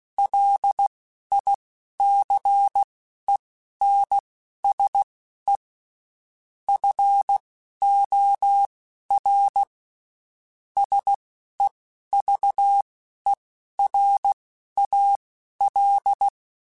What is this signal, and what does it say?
Morse code, heard as a keyed tone, about 16 wpm, spaced as for 11 wpm, 800 Hz. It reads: LICENSE FOR SEVERAL
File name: (spaced) / QSOs Spaced-mp3s / WA7QOX 2 @15 sp